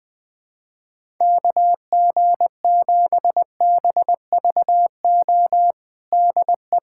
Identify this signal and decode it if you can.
Morse code, heard as a keyed tone, about 21 wpm, 700 Hz. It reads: KG7BVO DE
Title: USA Callsign DE Callsign - 6 Characters - 20wpm